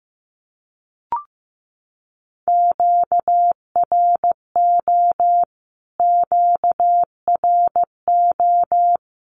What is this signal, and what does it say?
Morse code, heard as a keyed tone, about 15 wpm, 700 Hz. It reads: QRO QRO
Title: Q-Codes Repeated 3x 15wpm